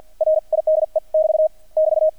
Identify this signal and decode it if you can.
Morse code, heard as a keyed tone, about 48 wpm, spaced as for 41 wpm, 630 Hz. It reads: RIDE <BT> <BT>
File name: archived / CRSnet